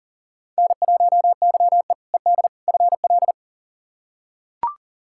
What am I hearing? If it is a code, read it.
Morse code, heard as a keyed tone, about 40 wpm, 700 Hz. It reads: N1YE ED FL
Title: K1USN SST QSO Practice - 40wpm